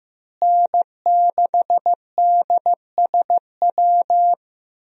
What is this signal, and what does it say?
Morse code, heard as a keyed tone, about 15 wpm, 700 Hz. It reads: N6DSW
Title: USA Call Signs - 5 Characters - Repeated 3x - 15wpm